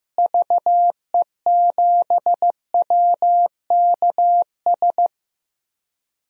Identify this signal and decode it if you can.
Morse code, heard as a keyed tone, about 15 wpm, 700 Hz. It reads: VE7WKS
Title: DXCC-100 Call Signs - 6 Characters - Repeated 3x - 15wpm